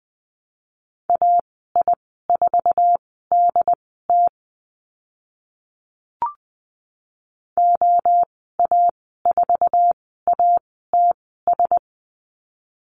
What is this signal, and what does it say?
Morse code, heard as a keyed tone, about 20 wpm, spaced as for 14 wpm, 700 Hz. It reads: AI4DT OA4ATS
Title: 069 Lesson 12g - Letter d - Call Signs 20wpm